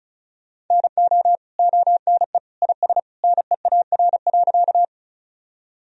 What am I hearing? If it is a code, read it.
Morse code, heard as a keyed tone, about 34 wpm, 700 Hz. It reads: NO ONE IS NEAR.
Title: Sentences from Top 1000 Words - Review Part 1 - 35wpm